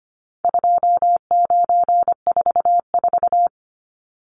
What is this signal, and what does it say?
Morse code, heard as a keyed tone, about 25 wpm, 700 Hz. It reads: 2944
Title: Numbers - 4-Digit 25wpm